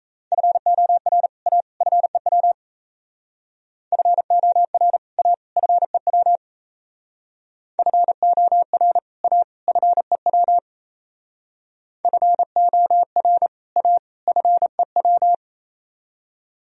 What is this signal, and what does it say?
Morse code, heard as a keyed tone, about 34 wpm, 700 Hz. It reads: FOR A FEW FOR A FEW FOR A FEW FOR A FEW
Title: Speed-Racing - Sets of 3 Words 28wpm